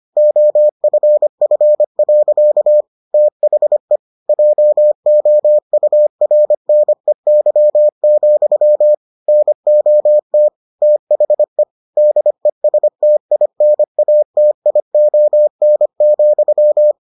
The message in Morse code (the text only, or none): OFF. THE JOURNEY, NOT THE DESTINATION,